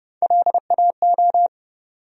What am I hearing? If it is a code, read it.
Morse code, heard as a keyed tone, about 32 wpm, 700 Hz. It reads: LAO
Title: USA Call Sign Postfixes - 3 Letters 30wpm